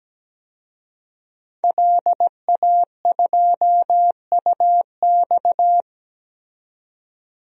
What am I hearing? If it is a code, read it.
Morse code, heard as a keyed tone, about 17 wpm, 700 Hz. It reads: LA2UX
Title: DXCC-100 Call Signs - 5 Characters - 17wpm